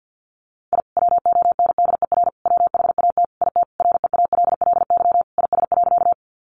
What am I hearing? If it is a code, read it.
Morse code, heard as a keyed tone, about 50 wpm, 700 Hz. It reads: I WONDER WHAT IT REALLY IS.
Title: Sentences from Top 600 Words 50wpm